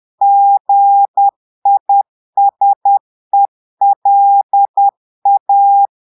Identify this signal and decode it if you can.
Morse code, heard as a keyed tone, about 10 wpm, 800 Hz. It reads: GISELA
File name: Portugais - texte en clair 1 - vitesse - 10 mots minute